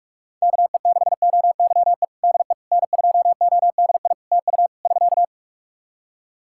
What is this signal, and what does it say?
Morse code, heard as a keyed tone, about 46 wpm, 700 Hz. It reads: KE6OYE DE N2ODI TU <SK>
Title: K1USN SST QSO Practice 45wpm